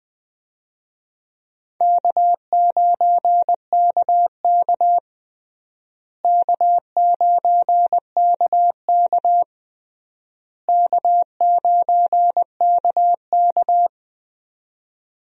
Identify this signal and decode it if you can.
Morse code, heard as a keyed tone, about 20 wpm, 700 Hz. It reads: K9KK K9KK K9KK
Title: USA Call Signs - 4 Characters - Repeated 3x - 20wpm